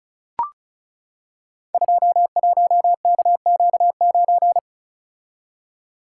Morse code, heard as a keyed tone, about 36 wpm, 700 Hz. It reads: 21KQ9
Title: Code Groups - 5 Letter:Numbers - 35wpm